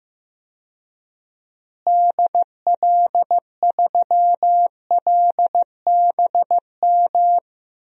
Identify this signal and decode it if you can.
Morse code, heard as a keyed tone, about 15 wpm, 700 Hz. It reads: DL3LBM
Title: DXCC-100 Call Signs - 6 Characters - Repeated 3x - 15wpm